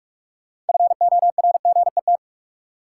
Morse code, heard as a keyed tone, about 46 wpm, 700 Hz. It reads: FORGET
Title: Sentences from Top 1200 Words - New Word Drill Alt - 45wpm